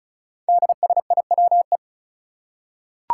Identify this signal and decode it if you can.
Morse code, heard as a keyed tone, about 36 wpm, 700 Hz. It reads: DSIWE
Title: Five Challenging Letter-Number Combinations 35wpm